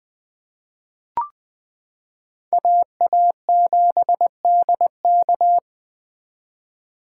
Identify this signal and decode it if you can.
Morse code, heard as a keyed tone, about 21 wpm, 700 Hz. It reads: AA7DK
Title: 2023 SOTA Chaser Call Signs - 20wpm